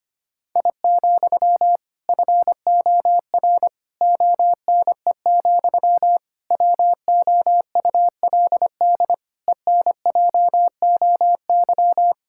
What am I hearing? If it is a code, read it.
Morse code, heard as a keyed tone, about 25 wpm, 700 Hz. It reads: I, FOR ONE, WOULD ENJOY
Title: Sentences from Top 1000 Words - Review Part 5 - 25wpm